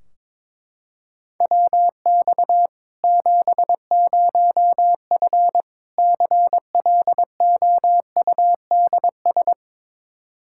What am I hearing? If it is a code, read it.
Morse code, heard as a keyed tone, about 22 wpm, 700 Hz. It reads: WX 70F CLOUDS
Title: Practice QSO - Exchange 2 - Info - 22wpm